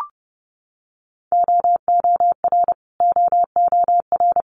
Morse code, heard as a keyed tone, about 32 wpm, 700 Hz. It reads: OOR OOR
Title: Multi-letter Phonemes Repeated 3x 30wpm